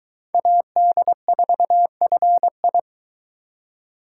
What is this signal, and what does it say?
Morse code, heard as a keyed tone, about 23 wpm, 700 Hz. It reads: AD4FI